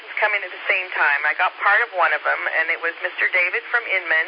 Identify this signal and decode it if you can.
Emergency Alert System (SAME) tones, heard as harsh buzzing bursts, with speech. It is heard from 0.9 to 3.6 s.
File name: SimplexLogic